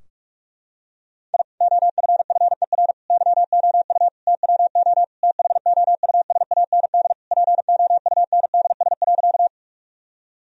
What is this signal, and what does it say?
Morse code, heard as a keyed tone, about 46 wpm, 700 Hz. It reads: I OFFER YOU TWO THOUSAND POUNDS.
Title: Sentences from Top 1000 Words - Review Part 4 - 45wpm